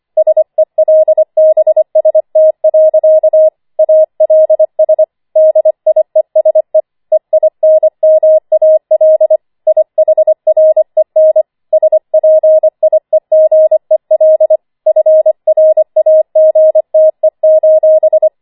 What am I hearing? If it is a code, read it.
Morse code, heard as a keyed tone, about 25 wpm, 630 Hz. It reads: SELBST. ALS DIESE EINMAL IHREN SPIEGEL FRAGTE:
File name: 25wpm